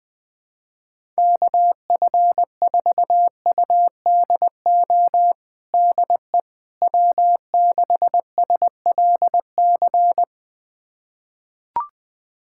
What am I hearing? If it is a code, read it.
Morse code, heard as a keyed tone, about 20 wpm, 700 Hz. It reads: KF4UDO DE W6SLC
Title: USA Callsign DE Callsign 20wpm